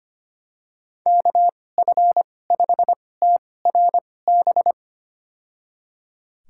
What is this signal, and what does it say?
Morse code, heard as a keyed tone, about 25 wpm, spaced as for 18 wpm, 700 Hz. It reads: KF5TRB
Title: USA Call Signs - 6 Characters - 2x Character Spacing - 25wpm